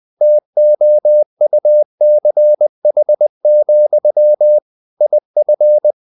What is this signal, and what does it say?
Morse code, heard as a keyed tone, about 20 wpm, 600 Hz. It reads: TOUCH, IF
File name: CW / Practice / If20